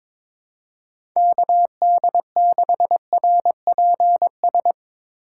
Morse code, heard as a keyed tone, about 22 wpm, 700 Hz. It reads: KD6RPS
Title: USA Call Signs - 6 Characters - 3x Character Spacing - 22wpm